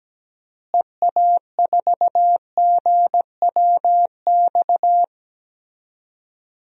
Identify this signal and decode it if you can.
Morse code, heard as a keyed tone, about 17 wpm, 700 Hz. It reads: EA4GWX
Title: DXCC-100 Call Signs - 6 Characters - Repeated 2x - 17wpm